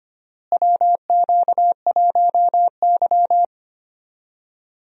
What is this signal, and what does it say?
Morse code, heard as a keyed tone, about 25 wpm, 700 Hz. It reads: WQ1Y
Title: USA Call Signs - 4 Characters - Repeated 2x - 25wpm